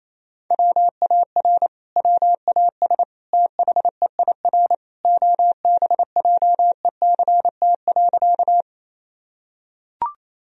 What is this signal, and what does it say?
Morse code, heard as a keyed tone, about 28 wpm, 700 Hz. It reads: WAR WAS THEIR OBJECT.
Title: Sentences from Top 1000 Words - Review Part 3 - 28wpm